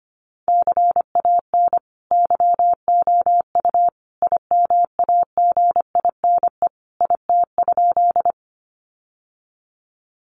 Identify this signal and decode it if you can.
Morse code, heard as a keyed tone, about 25 wpm, 700 Hz. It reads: CAN YOU IMAGINE IT?
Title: Sentences from Top 1400 Words 25wpm